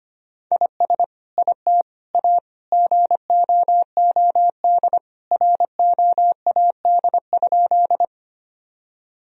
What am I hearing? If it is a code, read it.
Morse code, heard as a keyed tone, about 25 wpm, 700 Hz. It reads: IS IT A GOOD ROAD?
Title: Sentences from Top 1000 Words - Review Part 2 - 25wpm